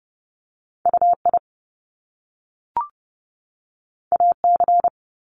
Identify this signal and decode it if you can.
Morse code, heard as a keyed tone, about 30 wpm, 700 Hz. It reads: UI AC
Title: Top Two Letter Combinations in Words 30wpm